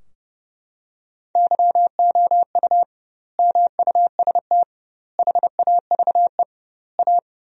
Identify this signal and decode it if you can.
Morse code, heard as a keyed tone, about 32 wpm, 700 Hz. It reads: YOU MUST HAVE A
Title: 2x Word-Spacing - Sentences from Top 500 Words - 30wpm